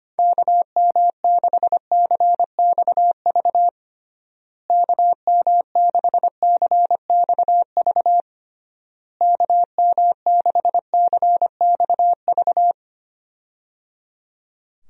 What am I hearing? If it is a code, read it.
Morse code, heard as a keyed tone, about 25 wpm, 700 Hz. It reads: KM6CXV KM6CXV KM6CXV
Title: USA Call Signs - 6 Characters - Repeated 3x - 25wpm